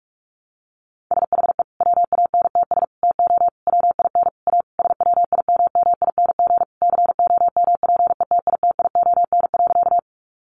Morse code, heard as a keyed tone, about 44 wpm, 700 Hz. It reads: SHE WANTS TO WIN A SWIMMING COMPETITION.
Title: Additional Easy Phrases 45wpm